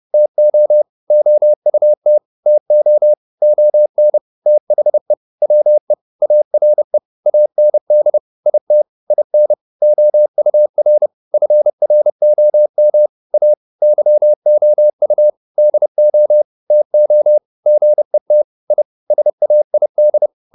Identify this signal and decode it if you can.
Morse code, heard as a keyed tone, about 30 wpm, 600 Hz. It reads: TO OUT TO ON THE WE ARE AND IT IN OUR FROM A YOU DO TO GET I SAID
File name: WD104 30